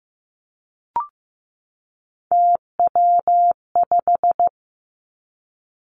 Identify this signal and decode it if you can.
Morse code, heard as a keyed tone, about 15 wpm, 700 Hz. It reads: TW5
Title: Three Challenging Letter-Number Combinations 15wpm